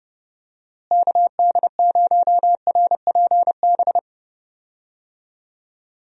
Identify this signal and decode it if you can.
Morse code, heard as a keyed tone, about 30 wpm, 700 Hz. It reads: KD0RPB
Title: USA Call Signs - 6 Characters - Repeated 3x - 30wpm